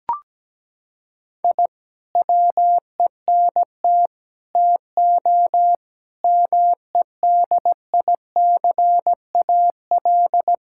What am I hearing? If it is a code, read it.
Morse code, heard as a keyed tone, about 17 wpm, 700 Hz. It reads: I WENT TO MEDICAL
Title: Sentences from Top 1000 Words - Review Part 5 - 17wpm